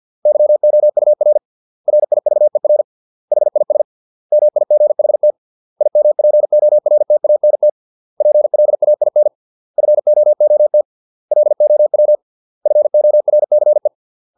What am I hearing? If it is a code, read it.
Morse code, heard as a keyed tone, about 50 wpm, 600 Hz. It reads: YOUR RIVER HIS MIGHT IMPORTANT PLAIN FOOT LOW FORCE